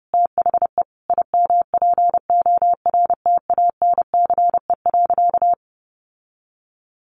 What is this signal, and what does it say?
Morse code, heard as a keyed tone, about 30 wpm, 700 Hz. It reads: THE IMPORTANCE.
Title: Sentences from Top 1200 Words 30wpm